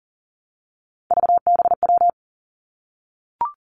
Morse code, heard as a keyed tone, about 40 wpm, 700 Hz. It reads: VBW
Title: Three Challenging Letter-Number Combinations 40wpm